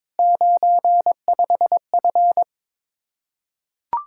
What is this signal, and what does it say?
Morse code, heard as a keyed tone, about 21 wpm, 700 Hz. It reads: 95F
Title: Practice QSO - Exchange 2 - Info - 22wpm